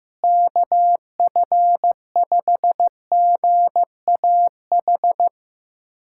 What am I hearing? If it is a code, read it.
Morse code, heard as a keyed tone, about 15 wpm, 700 Hz. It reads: KF5GAH
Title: USA Callsign DE Callsign 15wpm